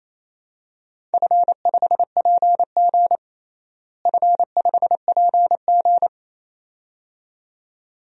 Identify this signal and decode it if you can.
Morse code, heard as a keyed tone, about 28 wpm, 700 Hz. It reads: F5PG F5PG
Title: DXCC-100 Call Signs - 4 Characters - Repeated 2x - 28wpm